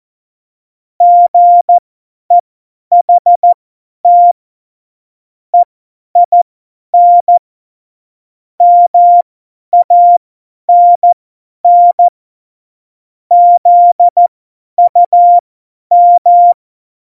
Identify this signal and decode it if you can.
Morse code, heard as a keyed tone, about 14 wpm, spaced as for 10 wpm, 700 Hz. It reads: GEHT EIN MANN ZUM
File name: Morsekurs / Witze / Witz 20 March 2018, 50 F70